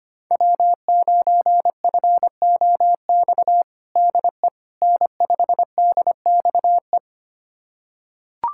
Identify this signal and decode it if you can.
Morse code, heard as a keyed tone, about 25 wpm, 700 Hz. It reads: W9FOX DE N5DXE
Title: USA Callsign DE Callsign - 5 Characters - 25wpm